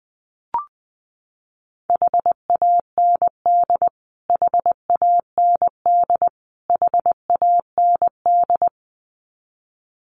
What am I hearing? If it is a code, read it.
Morse code, heard as a keyed tone, about 20 wpm, 700 Hz. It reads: HAND HAND HAND
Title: Top 500 Words Repeated 20wpm